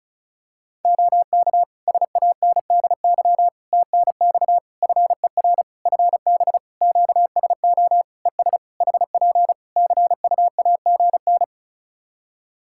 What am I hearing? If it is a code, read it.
Morse code, heard as a keyed tone, about 36 wpm, 700 Hz. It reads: OK SANDY TNX FER FB QSO ES HP CUAGN
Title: K1USN SST QSO Practice 35wpm